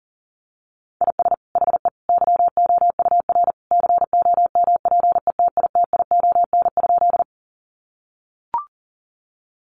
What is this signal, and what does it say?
Morse code, heard as a keyed tone, about 40 wpm, 700 Hz. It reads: IS HE YOUR COMPETITION?
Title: Sentences from Top 1000 Words 40wpm